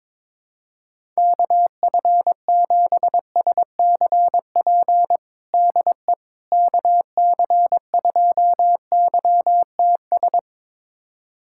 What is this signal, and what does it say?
Morse code, heard as a keyed tone, about 21 wpm, 700 Hz. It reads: KF7SCP DE KC2YTS
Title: USA Callsign DE Callsign - 6 Characters - 22wpm